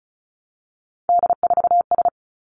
Morse code, heard as a keyed tone, about 36 wpm, 700 Hz. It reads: D4S
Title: Three Challenging Letter-Number Combinations 35wpm